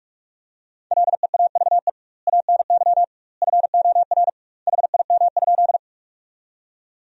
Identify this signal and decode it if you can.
Morse code, heard as a keyed tone, about 42 wpm, 700 Hz. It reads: LEAVE ANY FOR HIM?
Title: 2x Word-Spacing - Sentences from Top 400 Words - 45wpm